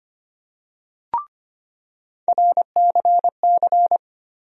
Morse code, heard as a keyed tone, about 25 wpm, 700 Hz. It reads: RCC
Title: USA Call Sign Postfixes - 3 Letters 25wpm